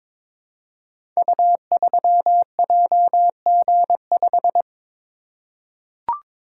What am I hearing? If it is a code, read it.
Morse code, heard as a keyed tone, about 22 wpm, 700 Hz. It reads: U3JG5
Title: Code Groups - 5 Letter:Numbers - 22wpm